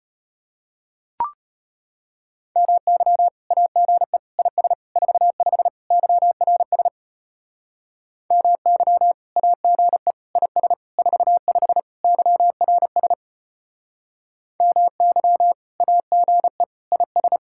Morse code, heard as a keyed tone, about 33 wpm, 700 Hz. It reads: MY AGE IS 45 YRS MY AGE IS 45 YRS MY AGE IS